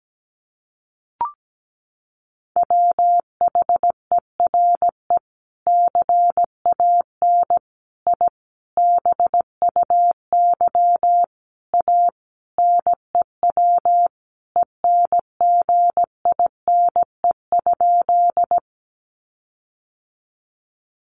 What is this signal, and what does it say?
Morse code, heard as a keyed tone, about 17 wpm, 700 Hz. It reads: WHERE CAN I BUY A NEW ENGINE?
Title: Sentences from Top 1500 Words 17wpm